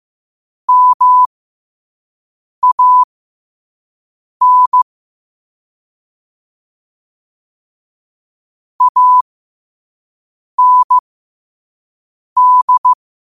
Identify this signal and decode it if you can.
Morse code, heard as a keyed tone, about 15 wpm, spaced as for 5 wpm, 1000 Hz. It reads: MAN AND
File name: qotdfiles / 5wpm